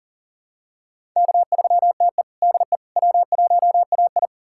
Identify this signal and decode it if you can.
Morse code, heard as a keyed tone, about 40 wpm, 700 Hz. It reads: K3TE DE W1AI
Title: USA Callsign DE Callsign - 4 Characters - 40wpm